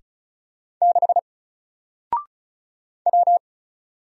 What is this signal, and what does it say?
Morse code, heard as a keyed tone, about 36 wpm, 700 Hz. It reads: 6 W
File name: Single-Letter-Number-35wpm